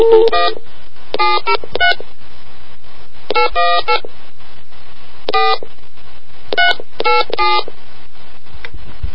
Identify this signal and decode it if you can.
DTMF, heard as telephone touch tones, tones of about 170 ms. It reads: 1 703 818 4 387